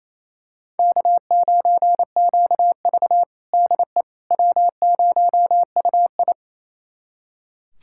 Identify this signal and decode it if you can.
Morse code, heard as a keyed tone, about 28 wpm, 700 Hz. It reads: K9QV DE W0UI
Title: USA Callsign DE Callsign - 4 Characters - 28wpm